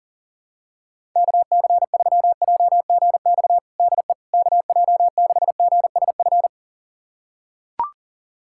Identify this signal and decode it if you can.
Morse code, heard as a keyed tone, about 40 wpm, 700 Hz. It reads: KC3JGX DE KJ6GSF
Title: USA Callsign DE Callsign - 6 Characters - 40wpm